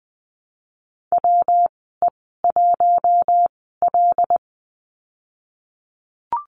Morse code, heard as a keyed tone, about 20 wpm, spaced as for 15 wpm, 700 Hz. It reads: WE1L
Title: 122 Lesson 20g - Number 3 - Call Signs 20wpm